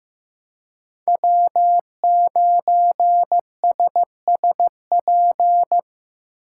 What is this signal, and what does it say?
Morse code, heard as a keyed tone, about 18 wpm, 700 Hz. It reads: W9SSP
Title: Speed Racing - USA 1x3 Call Signs 15wpm